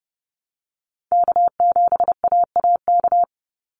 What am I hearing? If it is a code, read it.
Morse code, heard as a keyed tone, about 30 wpm, 700 Hz. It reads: K7AAK
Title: USA Call Signs 30wpm